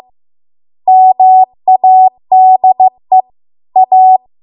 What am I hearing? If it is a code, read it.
Morse code, heard as a keyed tone, about 15 wpm, 750 Hz. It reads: MADE A